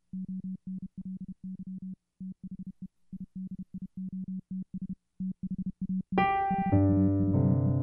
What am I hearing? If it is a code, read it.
Morse code, heard as a keyed tone, about 31 wpm, 190 Hz. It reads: ONLY THE IDIOTS THAT HAVE
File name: Cooper-Moore / A Retrospective 1990-2010 / Cooper-Moore - 05 - e If Only piano and morse code